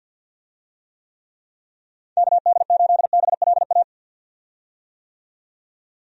Morse code, heard as a keyed tone, about 52 wpm, 700 Hz. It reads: KD8BLA
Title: USA Call Signs - 6 Characters - 50wpm